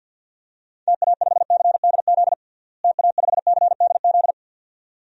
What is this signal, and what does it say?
Morse code, heard as a keyed tone, about 53 wpm, 700 Hz. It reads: TA5CDZ TA5CDZ
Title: DXCC-100 Call Signs - 6 Characters - Repeated 2x - 50wpm